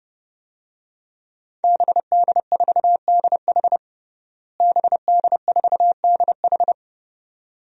Morse code, heard as a keyed tone, about 31 wpm, 700 Hz. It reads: BD4DH BD4DH
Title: DXCC-100 Call Signs - 5 Characters - Repeated 2x - 30wpm